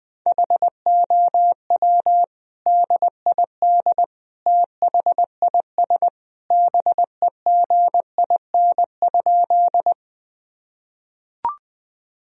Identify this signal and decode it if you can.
Morse code, heard as a keyed tone, about 21 wpm, 700 Hz. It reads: HOW DID THIS BEGIN?
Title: Sentences from Top 1000 Words - Review Part 1 - 20wpm